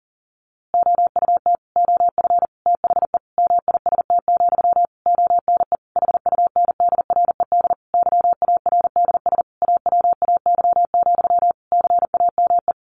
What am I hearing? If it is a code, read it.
Morse code, heard as a keyed tone, about 40 wpm, 700 Hz. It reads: OUT OF THE MIST, ONE HUNDRED YARDS AWAY, CAME
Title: A Sound Of Thunder 40wpm